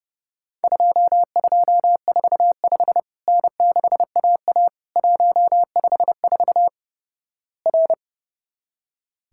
Morse code, heard as a keyed tone, about 30 wpm, 700 Hz. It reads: 2245 N6AA 154 R
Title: Worked All Europe DX Contest - QTC Exchange - 30wpm